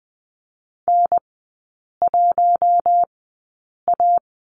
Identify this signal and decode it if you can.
Morse code, heard as a keyed tone, about 20 wpm, spaced as for 8 wpm, 700 Hz. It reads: N1A
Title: 041 Lesson 8e - Number 1 - Three Characters 20wpm